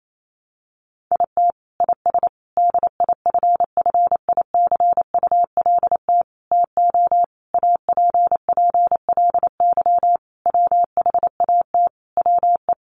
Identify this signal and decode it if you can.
Morse code, heard as a keyed tone, about 28 wpm, 700 Hz. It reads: IT IS DIFFICULT TO APPLY WHAT WE
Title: Sentences from Top 1200 Words 28wpm